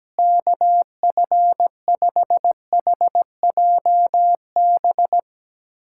Morse code, heard as a keyed tone, about 17 wpm, 700 Hz. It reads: KF5HJB
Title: USA Callsign DE Callsign - 6 Characters - 17wpm